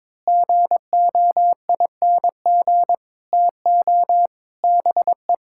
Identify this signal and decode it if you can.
Morse code, heard as a keyed tone, about 22 wpm, 700 Hz. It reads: GOING TO BE
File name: practice / 22wpm / Build A Phrase - 5-Word Phrases - 22wpm